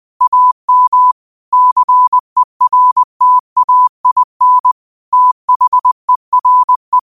Morse code, heard as a keyed tone, about 20 wpm, 1000 Hz. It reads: AM CERTAIN THERE
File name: qotdfiles / 20wpm